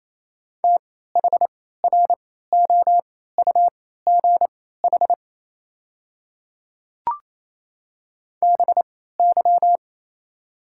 Morse code, heard as a keyed tone, about 28 wpm, spaced as for 16 wpm, 700 Hz. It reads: THROUGH BY
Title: 3x Character Spacing - Top 300 Words - 28wpm